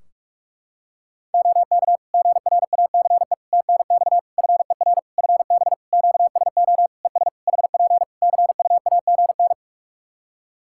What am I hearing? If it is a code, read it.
Morse code, heard as a keyed tone, about 46 wpm, 700 Hz. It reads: OK GRACE TNX FER FB QSO ES HP CUAGN
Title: Practice QSO - Exchange 4 - Ending - 45wpm